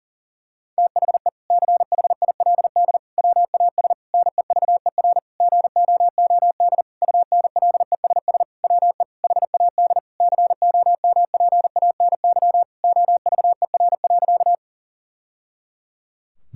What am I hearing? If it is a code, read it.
Morse code, heard as a keyed tone, about 37 wpm, 700 Hz. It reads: THE CHILD WAS NEVER GOOD UNLESS WE HAD COMPANY OVER.
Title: Sentences from Top 1000 Words - Review Part 6 - 40wpm